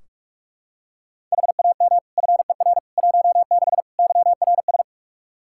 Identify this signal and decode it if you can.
Morse code, heard as a keyed tone, about 41 wpm, 700 Hz. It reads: HAM FER 16 YRS
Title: Practice QSO - Exchange 3 - More Info - 45wpm